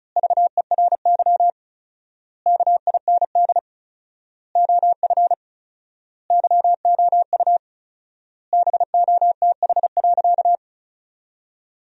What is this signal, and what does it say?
Morse code, heard as a keyed tone, about 36 wpm, 700 Hz. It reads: VERY KIND OF YOU BOTH.
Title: 4x Word-Spacing - Sentences from Top 300 Words - 35wpm